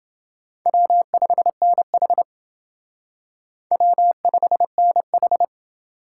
Morse code, heard as a keyed tone, about 29 wpm, 700 Hz. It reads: W5NH W5NH